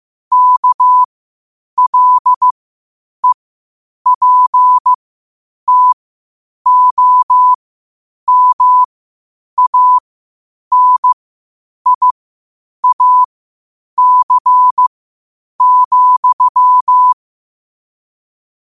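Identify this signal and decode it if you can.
Morse code, heard as a keyed tone, about 15 wpm, spaced as for 8 wpm, 1000 Hz. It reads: KLEPTOMANIAC,
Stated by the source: Morse code at 8 wpm.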